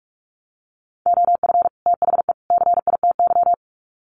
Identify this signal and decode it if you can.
Morse code, heard as a keyed tone, about 44 wpm, 700 Hz. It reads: OF THE CITY
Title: Sets of 3 Words 45wpm